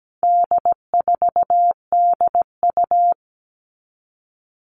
Morse code, heard as a keyed tone, about 17 wpm, 700 Hz. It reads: D4DU
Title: All Cut Numbers - 4-Digit 17wpm